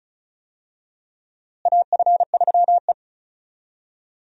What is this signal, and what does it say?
Morse code, heard as a keyed tone, about 36 wpm, 700 Hz. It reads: AF3E
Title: USA Call Signs - 4 Characters - Repeated 3x - 35wpm